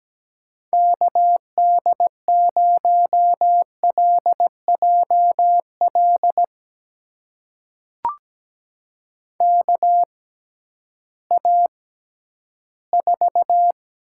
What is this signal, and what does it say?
Morse code, heard as a keyed tone, about 17 wpm, spaced as for 11 wpm, 700 Hz. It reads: KD0LJL KA4
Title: USA Call Signs - 6 Characters - 6x Character Spacing - 17wpm